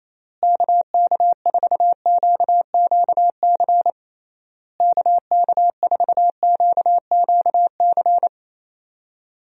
Morse code, heard as a keyed tone, about 28 wpm, 700 Hz. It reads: KK4QQC KK4QQC
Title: USA Call Signs - 6 Characters - Repeated 3x - 28wpm